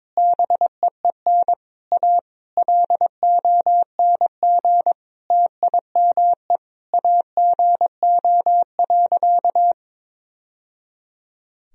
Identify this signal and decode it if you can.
Morse code, heard as a keyed tone, about 22 wpm, 700 Hz. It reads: BEEN A LONG TIME AGO.
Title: Sentences from Top 1000 Words - Review Part 3 - 22wpm